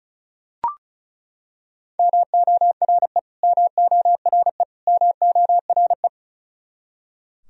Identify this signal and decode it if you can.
Morse code, heard as a keyed tone, about 34 wpm, 700 Hz. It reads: MORE MORE MORE
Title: Top 100 Words Repeated 35wpm